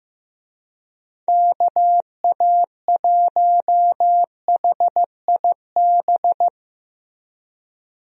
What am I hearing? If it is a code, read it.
Morse code, heard as a keyed tone, about 16 wpm, 700 Hz. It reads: KA1HIB
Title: USA Call Signs - 6 Characters - Repeated 2x - 15wpm